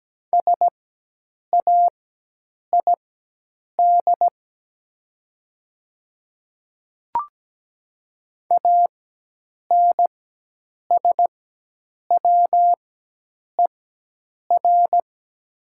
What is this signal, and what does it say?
Morse code, heard as a keyed tone, about 17 wpm, spaced as for 8 wpm, 700 Hz. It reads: SAID ANSWER
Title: 4x Character Spacing - Top 500 Words - 17wpm